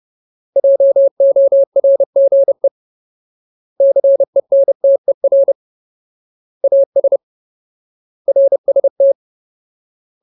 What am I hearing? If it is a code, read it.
Morse code, heard as a keyed tone, about 30 wpm, 550 Hz. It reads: JORGE CENTER AS RST
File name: Ultimate Word List / 30WPM Ultimate